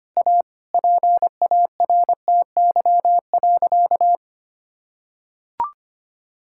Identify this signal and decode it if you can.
Morse code, heard as a keyed tone, about 25 wpm, 700 Hz. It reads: A PARTY.
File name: practice / 25wpm / Sentences with 4-Word Phrases - 25wpm